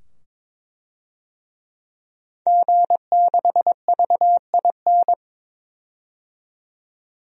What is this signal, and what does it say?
Morse code, heard as a keyed tone, about 22 wpm, 700 Hz. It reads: G6VIN
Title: DXCC-100 Call Signs - 5 Characters - 22wpm